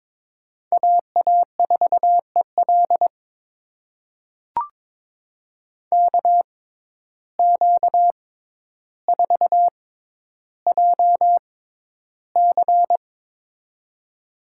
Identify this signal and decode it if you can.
Morse code, heard as a keyed tone, about 22 wpm, spaced as for 11 wpm, 700 Hz. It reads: AA4EL KQ4JC
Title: USA Call Signs - 5 Characters - 6x Character Spacing - 22wpm